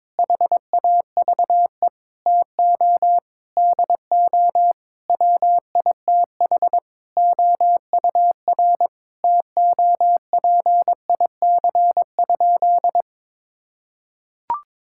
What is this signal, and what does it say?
Morse code, heard as a keyed tone, about 22 wpm, 700 Hz. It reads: HAVE TO DO WITH OUR TOPIC?
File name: practice / 22wpm / Sentences with 8-Word Phrases - 22wpm